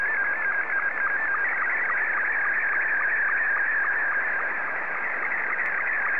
SSTV (slow-scan television), Scottie 2.
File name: sstv / S2-ZL2TJM